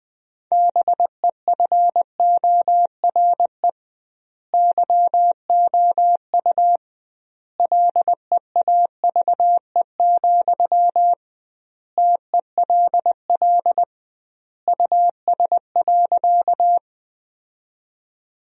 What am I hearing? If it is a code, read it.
Morse code, heard as a keyed tone, about 20 wpm, 700 Hz. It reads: BEFORE YOU LEAVE, TELL US.
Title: 2x Word-Spacing - Sentences from Top 400 Words - 20wpm